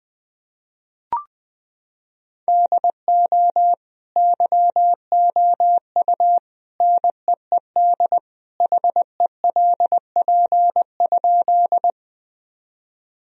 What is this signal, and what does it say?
Morse code, heard as a keyed tone, about 20 wpm, 700 Hz. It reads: DO YOU NEED HELP?
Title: Sentences from Top 1000 Words - Review Part 1 - 20wpm